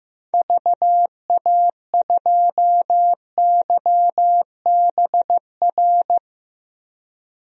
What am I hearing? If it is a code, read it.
Morse code, heard as a keyed tone, about 15 wpm, 700 Hz. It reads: VA2YBR
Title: DXCC-100 Call Signs - 6 Characters - 15wpm